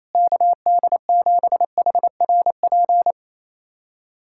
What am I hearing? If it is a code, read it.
Morse code, heard as a keyed tone, about 28 wpm, 700 Hz. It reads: KD7HRP
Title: USA Callsign DE Callsign 28wpm